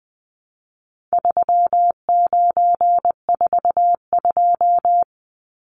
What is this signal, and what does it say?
Morse code, heard as a keyed tone, about 20 wpm, 700 Hz. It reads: 3942
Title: Numbers - 4-Digit 20wpm